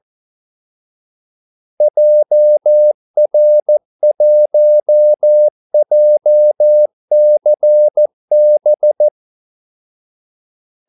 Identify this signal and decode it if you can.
Morse code, heard as a keyed tone, about 14 wpm, 600 Hz. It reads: JR1JCB
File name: Callsigns-14wpm